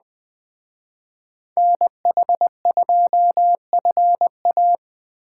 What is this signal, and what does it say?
Morse code, heard as a keyed tone, about 21 wpm, 700 Hz. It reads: NH2FA
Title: Speed Racing - USA Call Signs 17wpm